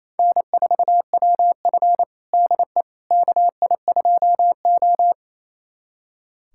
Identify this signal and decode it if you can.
Morse code, heard as a keyed tone, about 28 wpm, 700 Hz. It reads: N4WF DE KI2O
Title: USA Callsign DE Callsign - 4 Characters - 28wpm